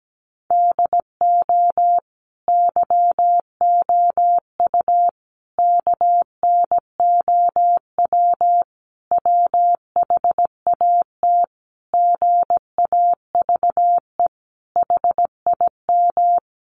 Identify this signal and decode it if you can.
Morse code, heard as a keyed tone, about 17 wpm, 700 Hz. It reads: DO YOU KNOW WHAT GAVE HIM
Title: Sentences from Top 1300 Words 17wpm